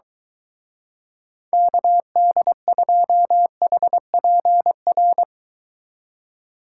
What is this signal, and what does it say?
Morse code, heard as a keyed tone, about 23 wpm, 700 Hz. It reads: KD2HPR